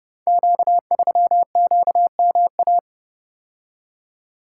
Morse code, heard as a keyed tone, about 30 wpm, 700 Hz. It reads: Q3QMA
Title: Code Groups - 5 Letter:Numbers - 30wpm